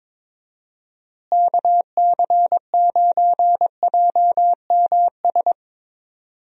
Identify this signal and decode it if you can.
Morse code, heard as a keyed tone, about 22 wpm, 700 Hz. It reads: KC9JMS
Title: USA Call Signs - 6 Characters - Repeated 2x - 22wpm